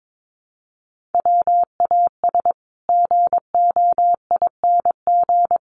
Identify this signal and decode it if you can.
Morse code, heard as a keyed tone, about 22 wpm, 700 Hz. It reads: WAS GOING
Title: Sets of 5 Words Encore 22wpm